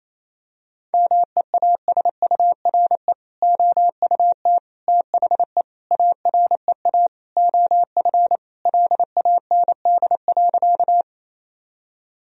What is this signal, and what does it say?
Morse code, heard as a keyed tone, about 28 wpm, 700 Hz. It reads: MEASURE OUT THE AREA OF LAND.
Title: Sentences from Top 1000 Words - Review Part 2 - 28wpm